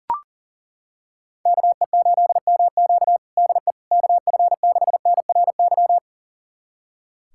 Morse code, heard as a keyed tone, about 40 wpm, 700 Hz. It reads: KE8MQ DE KF6NRY
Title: USA Callsign DE Callsign 40wpm